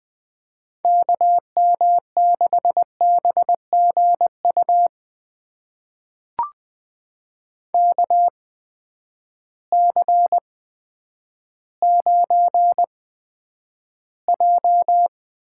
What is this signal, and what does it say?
Morse code, heard as a keyed tone, about 20 wpm, spaced as for 10 wpm, 700 Hz. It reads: KM6BGU KC9J
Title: USA Call Signs - 6 Characters - 8x Character Spacing - 20wpm